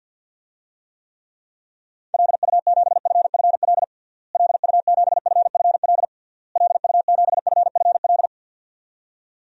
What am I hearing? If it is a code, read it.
Morse code, heard as a keyed tone, about 52 wpm, 700 Hz. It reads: LU7FFL LU7FFL LU7FFL
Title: DXCC-100 Call Signs - 6 Characters - Repeated 3x - 50wpm